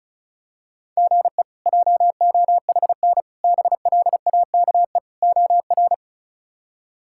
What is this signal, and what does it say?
Morse code, heard as a keyed tone, about 34 wpm, 700 Hz. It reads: GE JOHN BLAKE OR
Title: K1USN SST QSO Practice - 35wpm